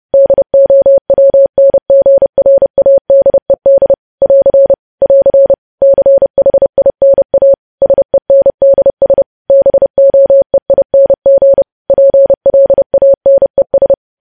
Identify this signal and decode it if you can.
Morse code, heard as a keyed tone, about 30 wpm, 550 Hz. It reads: DOWNGRADED <AR> <AR> CHINA SENDS BOEING PLANES
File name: podcast / News headlines-1x-30x30